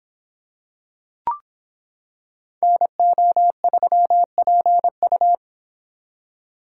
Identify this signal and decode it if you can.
Morse code, heard as a keyed tone, about 26 wpm, 700 Hz. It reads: NO3PU